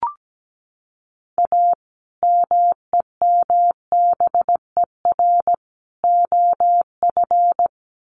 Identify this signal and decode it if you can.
Morse code, heard as a keyed tone, about 17 wpm, 700 Hz. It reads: A MEMBER OF
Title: Sets of 3 Words 17wpm